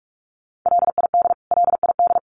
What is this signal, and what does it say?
Morse code, heard as a keyed tone, about 44 wpm, 700 Hz. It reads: LID LID
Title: Abbreviations Repeated 3x - 45wpm